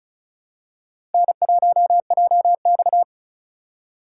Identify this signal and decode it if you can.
Morse code, heard as a keyed tone, about 36 wpm, 700 Hz. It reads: N1JX
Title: USA Call Signs - 2x Character Spacing - 35wpm